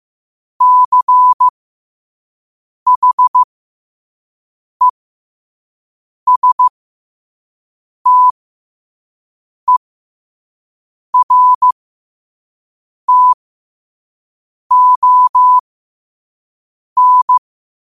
Morse code, heard as a keyed tone, about 15 wpm, spaced as for 5 wpm, 1000 Hz. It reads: CHESTERTON
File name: qotdfiles / 5wpm